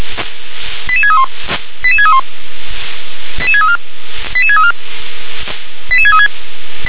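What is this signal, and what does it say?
Five-tone selective call (ZVEI), heard as quick runs of five tones, tones of about 70 ms. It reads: 88631 88631 88634 88634 88636
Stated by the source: ZVEI five-tone selective call 88510 received with an SDR.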